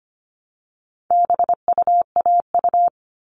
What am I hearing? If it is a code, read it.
Morse code, heard as a keyed tone, about 25 wpm, 700 Hz. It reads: BUAU
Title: All Cut Numbers - 4-Digit 25wpm